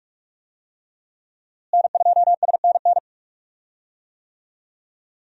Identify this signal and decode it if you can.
Morse code, heard as a keyed tone, about 46 wpm, 700 Hz. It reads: N2SNN
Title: USA Call Signs - 5 Characters - Repeated 3x - 45wpm